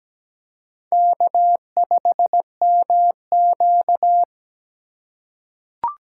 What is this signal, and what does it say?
Morse code, heard as a keyed tone, about 17 wpm, 700 Hz. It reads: K5MQ
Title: USA Call Signs - 4x Character Spacing - 17wpm